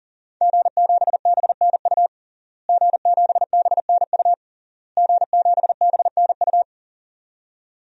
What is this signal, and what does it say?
Morse code, heard as a keyed tone, about 40 wpm, 700 Hz. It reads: G7BNU G7BNU G7BNU
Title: DXCC-100 Call Signs - 5 Characters - Repeated 3x - 40wpm